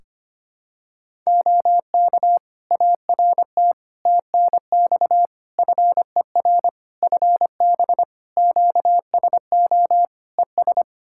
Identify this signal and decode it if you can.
Morse code, heard as a keyed tone, about 25 wpm, 700 Hz. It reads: OK ART TNX FER FB QSO ES
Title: Practice QSO - Exchange 4 - Ending - 25wpm